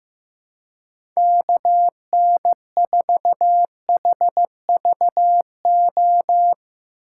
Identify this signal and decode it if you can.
Morse code, heard as a keyed tone, about 15 wpm, 700 Hz. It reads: KN4HVO
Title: USA Call Signs - 6 Characters - 2x Character Spacing - 15wpm